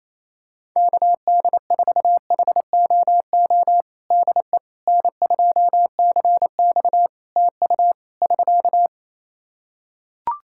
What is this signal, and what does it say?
Morse code, heard as a keyed tone, about 28 wpm, 700 Hz. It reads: KD4HOO DE N2CX TU <SK>
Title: Practice QSO - Exchange 4 - Ending - 28wpm